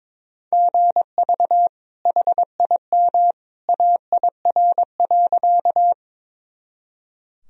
Morse code, heard as a keyed tone, about 22 wpm, 700 Hz. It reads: GV HIM AIR.
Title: Sentences from Top 200 Words and Abbreviations 22wpm